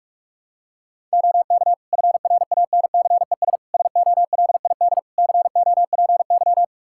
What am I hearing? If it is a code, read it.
Morse code, heard as a keyed tone, about 46 wpm, 700 Hz. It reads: OK FRANCES SOLID COPY
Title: Practice QSO - Exchange 3 - More Info - 45wpm